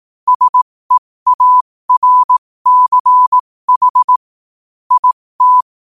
Morse code, heard as a keyed tone, about 18 wpm, spaced as for 15 wpm, 1000 Hz. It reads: SEARCH IT
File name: qotdfiles / 15wpm